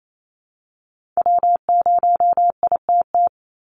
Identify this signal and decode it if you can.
Morse code, heard as a keyed tone, about 28 wpm, 700 Hz. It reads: W0ITT
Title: Call Signs Repeated 2x - 28wpm